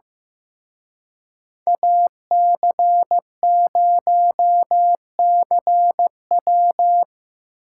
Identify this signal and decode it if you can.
Morse code, heard as a keyed tone, about 17 wpm, 700 Hz. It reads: AC0CW
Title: Speed Racing - USA 2x2 Call Signs 15wpm